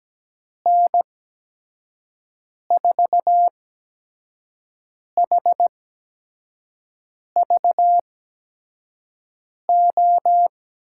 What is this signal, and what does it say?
Morse code, heard as a keyed tone, about 17 wpm, spaced as for 5 wpm, 700 Hz. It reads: N4HVO
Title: USA Call Signs - 8x Character Spacing - 17wpm